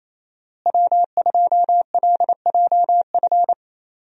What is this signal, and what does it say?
Morse code, heard as a keyed tone, about 28 wpm, 700 Hz. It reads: W2LJF
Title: USA Call Signs - 5 Characters - Repeated 2x - 28wpm